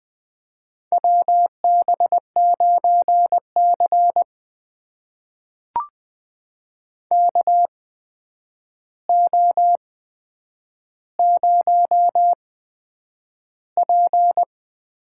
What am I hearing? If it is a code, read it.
Morse code, heard as a keyed tone, about 20 wpm, spaced as for 9 wpm, 700 Hz. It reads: WB9C KO0P
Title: USA Call Signs - 4 Characters - 8x Character Spacing - 20wpm